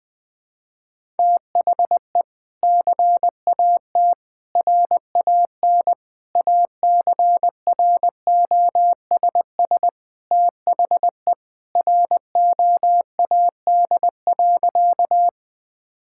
Morse code, heard as a keyed tone, about 20 wpm, 700 Hz. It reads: THE CAT RAN ACROSS THE ROAD.
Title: Sentences with 2-Word Phrases - 20wpm